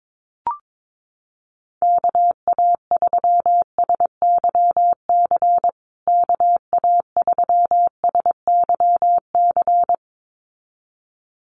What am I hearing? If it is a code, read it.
Morse code, heard as a keyed tone, about 22 wpm, 700 Hz. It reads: KA3SYC KA3SYC
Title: USA Call Signs Repeated 2x 22wpm